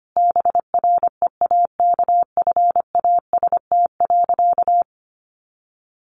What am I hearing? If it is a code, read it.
Morse code, heard as a keyed tone, about 25 wpm, 700 Hz. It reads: BREAKFAST.
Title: Easy Phrases 25wpm